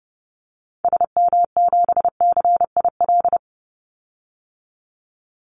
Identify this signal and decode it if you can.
Morse code, heard as a keyed tone, about 30 wpm, 700 Hz. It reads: SM7CIL
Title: Call Signs Repeated 2x - 30wpm